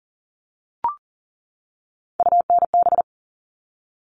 Morse code, heard as a keyed tone, about 40 wpm, 700 Hz. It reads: UNB
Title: All Cut Numbers - 3-Digit 40wpm